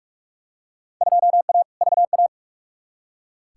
Morse code, heard as a keyed tone, about 46 wpm, 700 Hz. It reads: 2A VA
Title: Field Day Exchanges - 45wpm